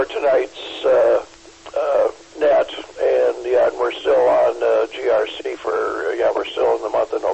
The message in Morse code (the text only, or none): none